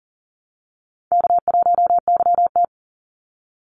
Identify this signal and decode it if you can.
Morse code, heard as a keyed tone, about 40 wpm, 700 Hz. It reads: K1YT
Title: Easy Call Signs Repeated 3x - 40wpm